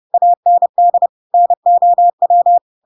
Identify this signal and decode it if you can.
Morse code, heard as a keyed tone, about 30 wpm, 700 Hz. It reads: AND NOW
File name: Pinocchio / Pinocchio ch7 30 30 ik1hge